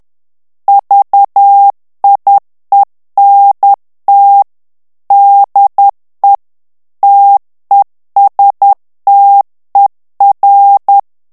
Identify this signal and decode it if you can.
Morse code, heard as a keyed tone, about 10 wpm, 800 Hz. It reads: VIENT DE TESTER
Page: Français - texte en clair 1 - vitesse : 10 mots minute